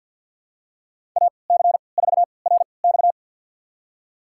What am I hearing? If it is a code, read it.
Morse code, heard as a keyed tone, about 46 wpm, spaced as for 27 wpm, 700 Hz. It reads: AC4RX
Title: USA Call Signs - 5 Characters - 3x Character Spacing - 50wpm